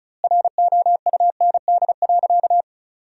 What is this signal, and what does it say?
Morse code, heard as a keyed tone, about 36 wpm, 700 Hz. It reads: ROUND.
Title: 2x Word-Spacing - Sentences from Top 200 Words - 35wpm